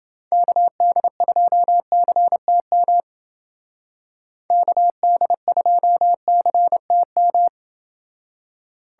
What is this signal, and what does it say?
Morse code, heard as a keyed tone, about 28 wpm, 700 Hz. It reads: KD2CTM KD2CTM